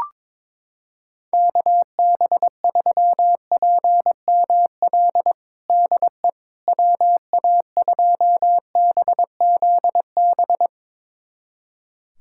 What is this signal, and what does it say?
Morse code, heard as a keyed tone, about 22 wpm, 700 Hz. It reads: KB3PML DE WA2BZB
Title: USA Callsign DE Callsign - 6 Characters - 22wpm